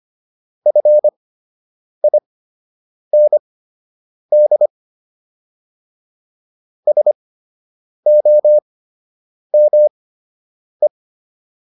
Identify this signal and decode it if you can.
Morse code, heard as a keyed tone, about 25 wpm, spaced as for 8 wpm, 600 Hz. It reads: FIND SOME